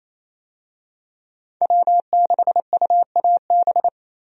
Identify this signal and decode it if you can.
Morse code, heard as a keyed tone, about 27 wpm, 700 Hz. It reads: W6UAB
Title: USA Call Signs - 5 Characters - Repeated 2x - 28wpm